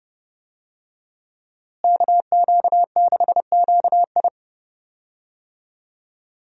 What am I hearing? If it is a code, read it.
Morse code, heard as a keyed tone, about 30 wpm, 700 Hz. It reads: KQ6QI
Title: USA Call Signs - 5 Characters - Repeated 2x - 30wpm